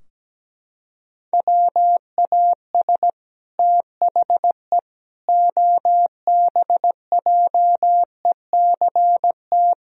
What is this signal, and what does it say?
Morse code, heard as a keyed tone, about 17 wpm, 700 Hz. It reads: WAS THE OBJECT
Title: Sentences from Top 1000 Words - Review Part 4 - 17wpm